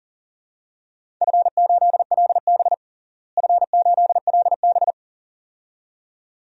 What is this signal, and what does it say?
Morse code, heard as a keyed tone, about 38 wpm, 700 Hz. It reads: F8LB F8LB
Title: DXCC-100 Call Signs - 4 Characters - Repeated 2x - 40wpm